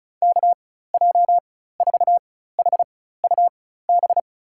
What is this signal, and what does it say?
Morse code, heard as a keyed tone, about 34 wpm, spaced as for 16 wpm, 700 Hz. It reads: KJ4HUB
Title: USA Call Signs - 6 Characters - 4x Character Spacing - 35wpm